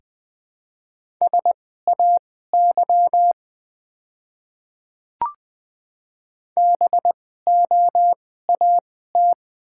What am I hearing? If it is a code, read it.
Morse code, heard as a keyed tone, about 20 wpm, spaced as for 14 wpm, 690 Hz. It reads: SAY BOAT
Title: 2x Character Spacing - Top 500 Words - 20wpm